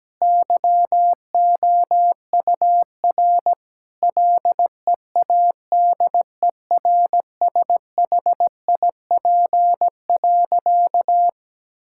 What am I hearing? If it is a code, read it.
Morse code, heard as a keyed tone, about 17 wpm, 700 Hz. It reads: YOUR LEADERSHIP.
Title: Sentences from Top 1600 Words 17wpm